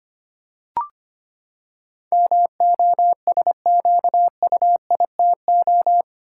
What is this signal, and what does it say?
Morse code, heard as a keyed tone, about 25 wpm, 700 Hz. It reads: MOSQUITO
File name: Eight Letter Words - 25wpm